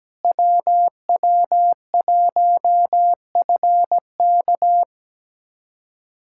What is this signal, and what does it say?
Morse code, heard as a keyed tone, about 17 wpm, 700 Hz. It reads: WW1FK
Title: USA Callsign DE Callsign - 5 Characters - 17wpm